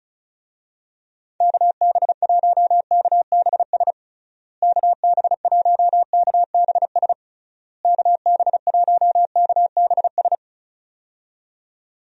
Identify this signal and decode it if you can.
Morse code, heard as a keyed tone, about 34 wpm, 700 Hz. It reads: KB1KBS KB1KBS KB1KBS
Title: USA Call Signs - 6 Characters - Repeated 3x - 35wpm